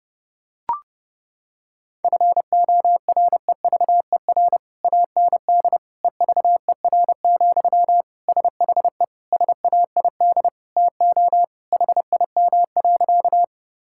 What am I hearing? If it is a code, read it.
Morse code, heard as a keyed tone, about 30 wpm, 700 Hz. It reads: FOREVER AND EVER, SHE SAID TO HIM.
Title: Sentences from Top 1600 Words 30wpm